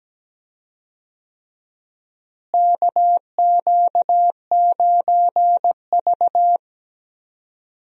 Morse code, heard as a keyed tone, about 17 wpm, 700 Hz. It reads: KQ9V
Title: USA Call Signs - 4 Characters - Repeated 3x - 17wpm